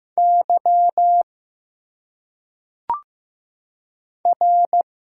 Morse code, heard as a keyed tone, about 15 wpm, 700 Hz. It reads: Y R
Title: Single Letters - Phonetic Pronunciation 15wpm